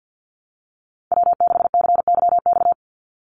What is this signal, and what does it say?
Morse code, heard as a keyed tone, about 48 wpm, 700 Hz. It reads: W6CYX
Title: Call Signs Repeated 3x - 50wpm